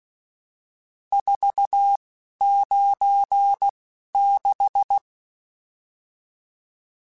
Morse code, heard as a keyed tone, about 16 wpm, spaced as for 11 wpm, 800 Hz. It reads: 496
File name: RSTs / RST-mp3s / Various RST @10